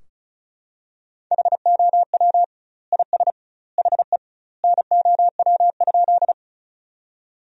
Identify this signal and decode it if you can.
Morse code, heard as a keyed tone, about 34 wpm, 700 Hz. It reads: HOW IS HE NOW?
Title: 2x Word-Spacing - Sentences from Top 100 Words - 35wpm